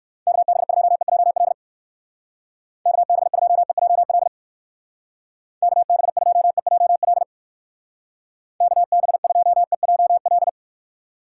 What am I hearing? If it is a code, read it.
Morse code, heard as a keyed tone, about 54 wpm, 700 Hz. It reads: KB2EJL KB2EJL KB2EJL KB2EJL
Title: Speed Racing - USA 2x3 Call Signs 45wpm